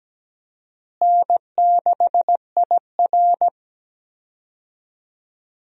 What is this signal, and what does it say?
Morse code, heard as a keyed tone, about 17 wpm, 700 Hz. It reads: N6IR
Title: USA Call Signs - 4 Characters - Repeated 2x - 17wpm